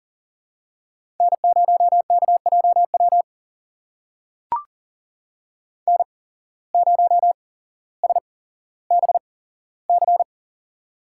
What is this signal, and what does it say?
Morse code, heard as a keyed tone, about 40 wpm, spaced as for 17 wpm, 700 Hz. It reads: N0KJW N0SBC
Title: USA Call Signs - 5 Characters - 8x Character Spacing - 40wpm